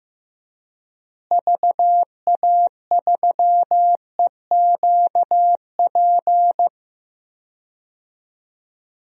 Morse code, heard as a keyed tone, about 15 wpm, 700 Hz. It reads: VA3EQP
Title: DXCC-100 Call Signs - 6 Characters - Repeated 3x - 15wpm